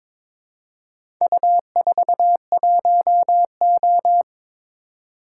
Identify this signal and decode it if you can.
Morse code, heard as a keyed tone, about 22 wpm, 700 Hz. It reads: U41O
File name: practice / 22wpm / Four Letter-Number Combinations - 22wpm